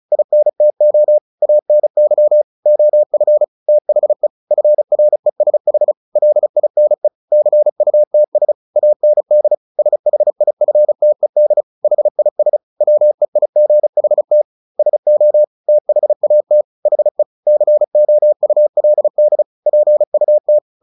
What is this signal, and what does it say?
Morse code, heard as a keyed tone, about 34 wpm, 600 Hz. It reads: INTO ANY OF THE FRESH LINE CUTS AND SHIFTED HIS WEIGHT SO THAT HE COULD PUT
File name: the old man and the sea 2 35wpm